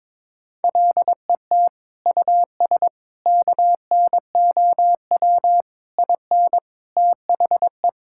Morse code, heard as a keyed tone, about 22 wpm, 700 Hz. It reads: LET US KNOW IN THE
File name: practice / 22wpm / Sets of 6 Words - 22wpm